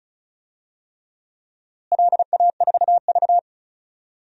Morse code, heard as a keyed tone, about 36 wpm, 700 Hz. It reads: LA4V
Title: DXCC-100 Call Signs - 4 Characters - Repeated 3x - 35wpm